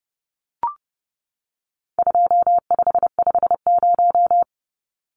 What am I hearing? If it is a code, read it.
Morse code, heard as a keyed tone, about 30 wpm, 700 Hz. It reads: 2550
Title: Numbers - 4-Digit 30wpm